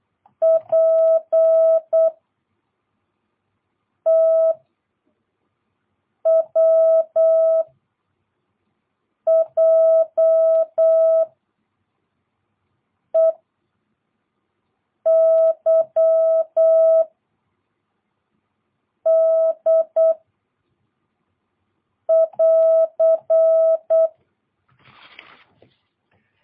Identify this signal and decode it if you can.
Morse code, heard as a keyed tone, about 8 wpm, 660 Hz. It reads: P T W J E Y D <AR>